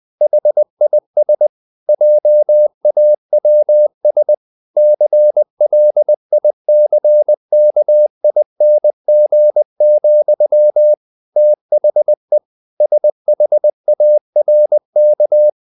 Morse code, heard as a keyed tone, about 20 wpm, 600 Hz. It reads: HIS JAWS CLICKING, THE SHARK
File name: the old man and the sea part 2 20wpm